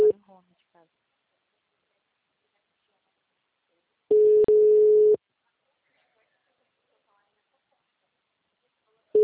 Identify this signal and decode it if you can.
Special information tone, with speech.